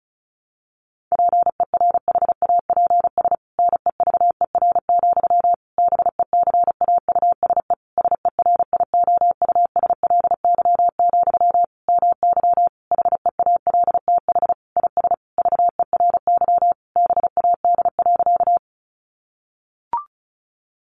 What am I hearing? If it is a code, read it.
Morse code, heard as a keyed tone, about 34 wpm, 700 Hz. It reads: PERHAPS NEVER, BECAUSE SERIOUSLY, MY HEALTH IS VERY BAD.
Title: Sentences from Top 1200 Words 35wpm